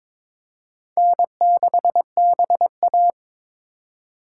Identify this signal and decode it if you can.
Morse code, heard as a keyed tone, about 21 wpm, 700 Hz. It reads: N6BA
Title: USA Call Signs - 4 Characters - 4x Character Spacing - 22wpm